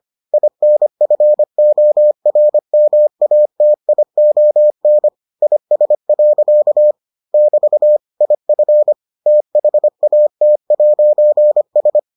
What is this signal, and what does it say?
Morse code, heard as a keyed tone, about 25 wpm, 600 Hz. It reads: INFORMATION IS. <BT> IF THAT'S